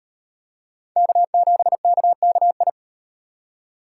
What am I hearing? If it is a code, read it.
Morse code, heard as a keyed tone, about 36 wpm, 700 Hz. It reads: K7KKI